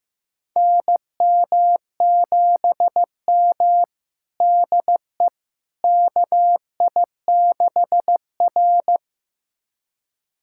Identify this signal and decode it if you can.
Morse code, heard as a keyed tone, about 15 wpm, 700 Hz. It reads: NM7M DE KI6R
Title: USA Callsign DE Callsign - 4 Characters - 15wpm